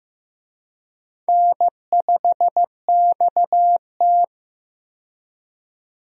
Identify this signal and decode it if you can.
Morse code, heard as a keyed tone, about 15 wpm, 700 Hz. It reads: N5XT
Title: USA Call Signs - 4 Characters - 15wpm